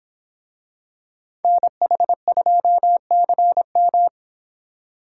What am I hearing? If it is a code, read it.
Morse code, heard as a keyed tone, about 25 wpm, 700 Hz. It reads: NH2CM